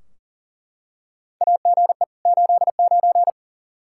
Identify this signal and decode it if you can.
Morse code, heard as a keyed tone, about 39 wpm, 700 Hz. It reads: AGE 89
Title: K1USN SST QSO Practice 40wpm